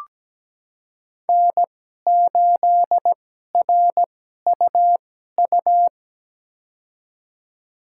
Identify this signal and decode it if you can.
Morse code, heard as a keyed tone, about 17 wpm, spaced as for 12 wpm, 700 Hz. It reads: N8RUU
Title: USA Call Signs - 5 Characters - 2x Character Spacing - 17wpm